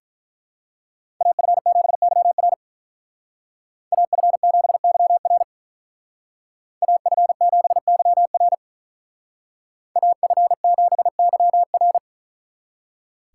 Morse code, heard as a keyed tone, about 44 wpm, 700 Hz. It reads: AF7YR AF7YR AF7YR AF7YR